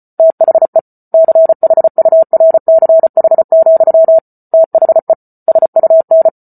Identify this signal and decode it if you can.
Morse code, heard as a keyed tone, about 34 wpm, 650 Hz. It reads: THE CHURCH, THE SUN